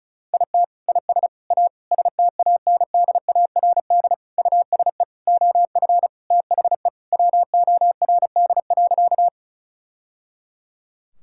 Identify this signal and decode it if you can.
Morse code, heard as a keyed tone, about 34 wpm, 700 Hz. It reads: IT IS A STANDARD USE OF THE WORD.
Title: Sentences from Top 1000 Words - Review Part 6 - 35wpm